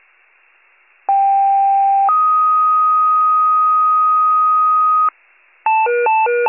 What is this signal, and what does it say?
Two-tone page, heard as a one-second tone, then a three-second tone, about 790 then 1250 Hz. It is heard from 1.1 to 5.1 s.